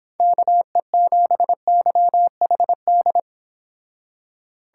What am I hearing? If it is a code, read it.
Morse code, heard as a keyed tone, about 26 wpm, 700 Hz. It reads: KE7YHD